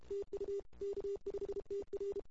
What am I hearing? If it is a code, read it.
Morse code, heard as a keyed tone, about 32 wpm, 410 Hz. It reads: TU K5TR
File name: CW / 2008-ss-cw-k5tr-op-n5rz -23:00